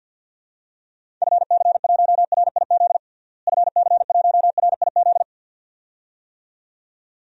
Morse code, heard as a keyed tone, about 51 wpm, 700 Hz. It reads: FC1RIZ FC1RIZ
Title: DXCC-100 Call Signs - 6 Characters - Repeated 2x - 50wpm